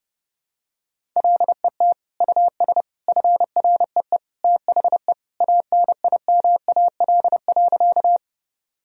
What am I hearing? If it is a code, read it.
Morse code, heard as a keyed tone, about 30 wpm, 700 Hz. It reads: LET US FREE THE ANIMAL.
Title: Sentences from Top 1000 Words - Review Part 1 - 30wpm